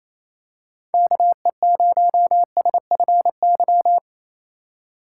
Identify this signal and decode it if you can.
Morse code, heard as a keyed tone, about 28 wpm, 700 Hz. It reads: KE0SFY
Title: USA Call Signs - 6 Characters - 2x Character Spacing - 28wpm